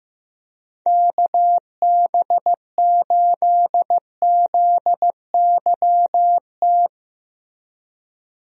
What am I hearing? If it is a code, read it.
Morse code, heard as a keyed tone, about 15 wpm, 700 Hz. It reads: KB8ZYT
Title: USA Call Signs - 6 Characters - Repeated 3x - 15wpm